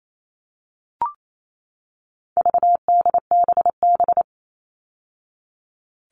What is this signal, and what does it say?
Morse code, heard as a keyed tone, about 28 wpm, 700 Hz. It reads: VDBB
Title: All Cut Numbers - 4-Digit 28wpm